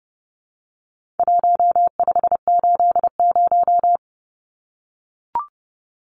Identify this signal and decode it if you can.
Morse code, heard as a keyed tone, about 30 wpm, 700 Hz. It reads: 1580
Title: Numbers - Contest Style - 4-Digit 30wpm